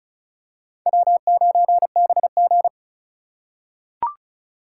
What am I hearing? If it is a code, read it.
Morse code, heard as a keyed tone, about 34 wpm, 700 Hz. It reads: W9BG
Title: USA Call Signs - 4x Character Spacing - 35wpm